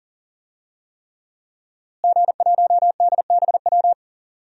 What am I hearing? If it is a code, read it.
Morse code, heard as a keyed tone, about 40 wpm, 700 Hz. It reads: G1DBW
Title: DXCC-100 Call Signs - 5 Characters - Repeated 3x - 40wpm